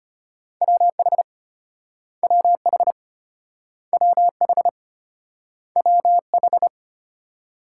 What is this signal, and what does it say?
Morse code, heard as a keyed tone, about 34 wpm, 700 Hz. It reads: WH WH WH WH